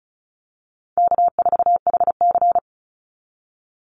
Morse code, heard as a keyed tone, about 34 wpm, 700 Hz. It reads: K4HC
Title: Easy Call Signs Repeated 3x - 35wpm